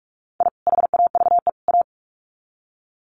Morse code, heard as a keyed tone, about 44 wpm, 700 Hz. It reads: I HAVE A
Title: Sets of 3 Words 45wpm